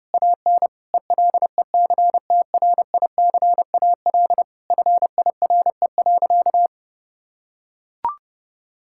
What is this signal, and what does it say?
Morse code, heard as a keyed tone, about 30 wpm, 700 Hz. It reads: AN ELECTRICAL FIRE.
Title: Sentences from Top 1000 Words - Review Part 6 - 30wpm